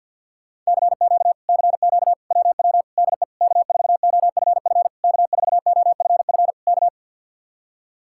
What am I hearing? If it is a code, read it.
Morse code, heard as a keyed tone, about 67 wpm, spaced as for 58 wpm, 700 Hz. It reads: CQ CQ WW DE K4OFF K4OFF K
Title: Speed-Racing Common QSO Phrases - 50wpm